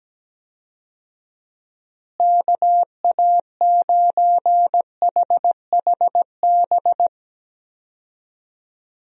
Morse code, heard as a keyed tone, about 17 wpm, 700 Hz. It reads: KA9HHB
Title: USA Call Signs - 6 Characters - Repeated 3x - 17wpm